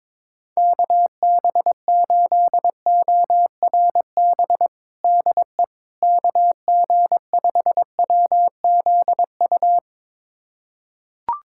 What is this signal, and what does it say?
Morse code, heard as a keyed tone, about 22 wpm, 700 Hz. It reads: KB8ORB DE KG5WZU
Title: USA Callsign DE Callsign - 6 Characters - 22wpm